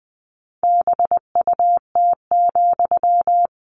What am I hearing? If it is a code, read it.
Morse code, heard as a keyed tone, about 20 wpm, 700 Hz. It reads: BUT,
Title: Sentences from Top 600 Words 20wpm